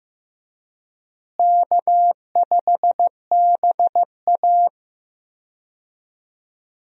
Morse code, heard as a keyed tone, about 15 wpm, 700 Hz. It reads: K5BA
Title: USA Call Signs - 4 Characters - Repeated 2x - 15wpm